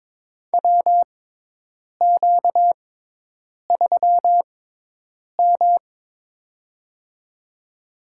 Morse code, heard as a keyed tone, about 22 wpm, spaced as for 8 wpm, 700 Hz. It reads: WQ3M
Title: USA Call Signs - 4 Characters - 6x Character Spacing - 22wpm